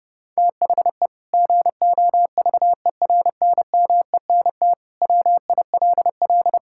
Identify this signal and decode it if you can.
Morse code, heard as a keyed tone, about 32 wpm, 700 Hz. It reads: THE GOVERNMENT WILL
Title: Sentences from Top 1000 Words - Review Part 5 - 30wpm